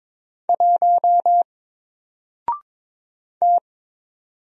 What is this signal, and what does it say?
Morse code, heard as a keyed tone, about 22 wpm, 700 Hz. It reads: 1 T
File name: Single-Letter-Number-22wpm